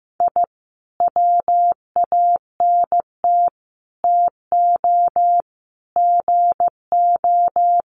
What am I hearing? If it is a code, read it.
Morse code, heard as a keyed tone, about 15 wpm, 700 Hz. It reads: I WANT TO GO
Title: Sets of 5 Words Encore 15wpm